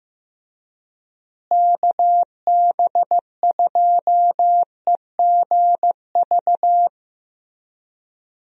Morse code, heard as a keyed tone, about 15 wpm, 700 Hz. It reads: KB2EGV
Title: USA Call Signs - 6 Characters - Repeated 3x - 15wpm